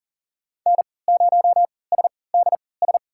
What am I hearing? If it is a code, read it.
Morse code, heard as a keyed tone, about 40 wpm, spaced as for 23 wpm, 700 Hz. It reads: N0SDS
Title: USA Call Signs - 3x Character Spacing - 40wpm